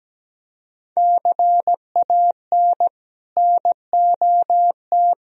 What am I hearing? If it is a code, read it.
Morse code, heard as a keyed tone, about 20 wpm, 700 Hz. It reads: CAN NOT
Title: Speed-Racing - Sets of 2 Words 17wpm